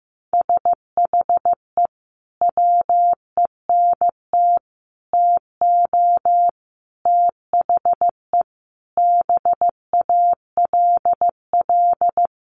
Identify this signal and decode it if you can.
Morse code, heard as a keyed tone, about 15 wpm, 700 Hz. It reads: SHE WENT TO THE BALL
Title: Sentences from Top 900 Words 15wpm